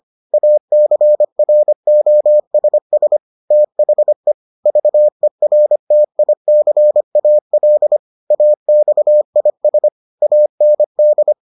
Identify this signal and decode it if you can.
Morse code, heard as a keyed tone, about 25 wpm, 600 Hz. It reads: ACROSS THE VERTICAL AXIS AND